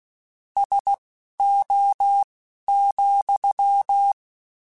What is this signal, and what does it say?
Morse code, heard as a keyed tone, about 16 wpm, spaced as for 12 wpm, 800 Hz. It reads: SO,
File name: QSOs / QSOs-mp3s / WD0BGP 1@15